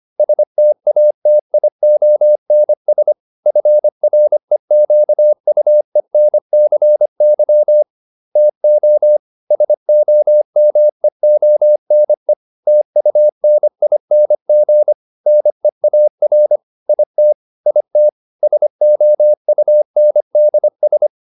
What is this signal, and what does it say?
Morse code, heard as a keyed tone, about 25 wpm, 600 Hz. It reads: STATIONS FREQUENCY TO SOMEONE TUNING NEAR IT IT SOUNDS